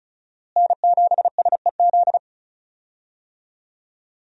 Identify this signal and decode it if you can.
Morse code, heard as a keyed tone, about 36 wpm, 700 Hz. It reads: N7SEZ
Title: USA Call Signs - 5 Characters - Repeated 2x - 35wpm